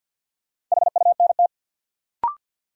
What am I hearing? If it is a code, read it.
Morse code, heard as a keyed tone, about 53 wpm, 700 Hz. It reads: HUNT
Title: Sentences from Top 1600 Words - New Word Drill - 50wpm